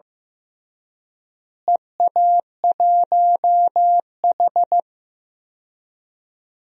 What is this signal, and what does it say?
Morse code, heard as a keyed tone, about 14 wpm, 700 Hz. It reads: EA1H
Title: DXCC-100 Call Signs - 4 Characters - Repeated 3x - 15wpm